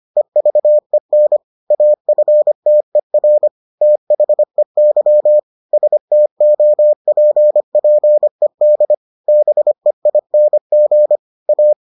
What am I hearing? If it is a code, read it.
Morse code, heard as a keyed tone, about 25 wpm, 600 Hz. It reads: EVEN AFTER THEY STOPPED BEING A